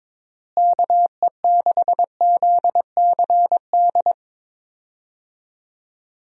Morse code, heard as a keyed tone, about 22 wpm, 700 Hz. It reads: KE6ZCD
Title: USA Call Signs - 6 Characters - Repeated 2x - 22wpm